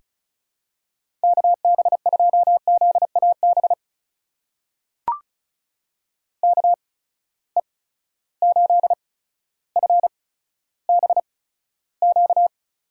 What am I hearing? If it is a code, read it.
Morse code, heard as a keyed tone, about 36 wpm, spaced as for 15 wpm, 700 Hz. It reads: KB2ZAB KE8FBQ
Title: USA Call Signs - 6 Characters - 8x Character Spacing - 35wpm